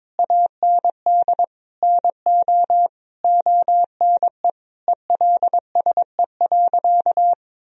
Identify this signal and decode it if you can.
Morse code, heard as a keyed tone, about 22 wpm, 700 Hz. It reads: AND NO ONE ELSE.
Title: Sentences from Top 1000 Words - Review Part 1 - 22wpm